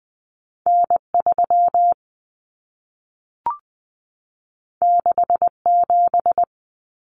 Morse code, heard as a keyed tone, about 20 wpm, 700 Hz. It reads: N3 67
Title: Common Cut Numbers - 2-Digit 20wpm